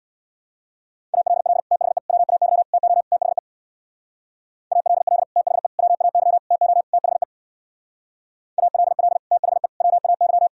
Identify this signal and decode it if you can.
Morse code, heard as a keyed tone, about 64 wpm, 700 Hz. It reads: ALL THE WAY TO THE ALL THE WAY TO THE ALL THE WAY